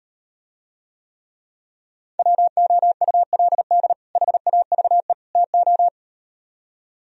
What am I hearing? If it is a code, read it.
Morse code, heard as a keyed tone, about 38 wpm, 700 Hz. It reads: WOULD HAVE TO